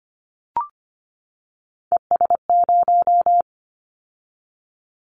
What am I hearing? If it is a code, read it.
Morse code, heard as a keyed tone, about 25 wpm, 700 Hz. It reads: ES0
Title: Callsign Prefixes for Top 50 DXCC - 25wpm